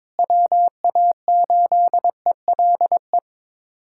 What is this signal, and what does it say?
Morse code, heard as a keyed tone, about 22 wpm, 700 Hz. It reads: WA8ELE
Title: USA Callsign DE Callsign - 6 Characters - 22wpm